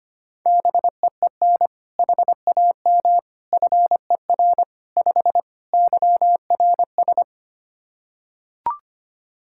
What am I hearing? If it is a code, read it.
Morse code, heard as a keyed tone, about 25 wpm, 700 Hz. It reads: BEEN HAM FER 5 YRS
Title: K1USN SST QSO Practice 25wpm